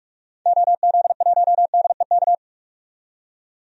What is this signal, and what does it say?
Morse code, heard as a keyed tone, about 44 wpm, 700 Hz. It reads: OZ1DEK
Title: DXCC-100 Call Signs - 6 Characters - 45wpm